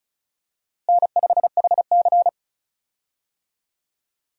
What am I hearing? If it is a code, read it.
Morse code, heard as a keyed tone, about 36 wpm, 700 Hz. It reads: N5HC
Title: USA Call Signs - 4 Characters - Repeated 3x - 35wpm